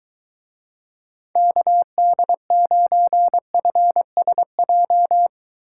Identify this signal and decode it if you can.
Morse code, heard as a keyed tone, about 23 wpm, 700 Hz. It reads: KD9FSJ